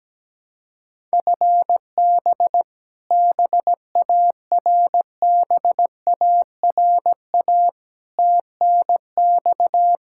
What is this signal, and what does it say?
Morse code, heard as a keyed tone, about 17 wpm, 700 Hz. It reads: FB BARBARA TNX
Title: Practice QSO - Exchange 3 - More Info - 17wpm